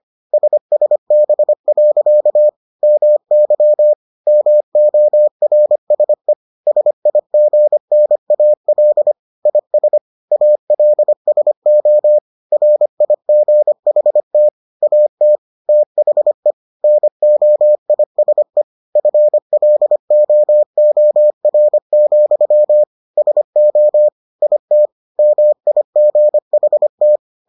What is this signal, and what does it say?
Morse code, heard as a keyed tone, about 25 wpm, 600 Hz. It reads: SSB. MY MORSE SIGNAL IS ALSO RIGHT AT THE NOISE FLOOR, SO IT MIGHT